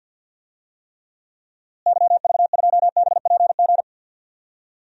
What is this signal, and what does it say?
Morse code, heard as a keyed tone, about 50 wpm, 700 Hz. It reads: YV2BPG
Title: DXCC-100 Call Signs - 6 Characters - Repeated 2x - 50wpm